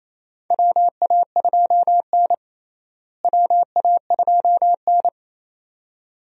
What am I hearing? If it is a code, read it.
Morse code, heard as a keyed tone, about 28 wpm, 700 Hz. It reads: WA2N WA2N
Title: USA Call Signs - 4 Characters - Repeated 2x - 28wpm